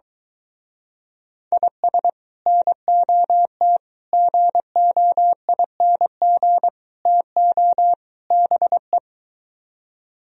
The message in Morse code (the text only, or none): IS NOT GOING TO BE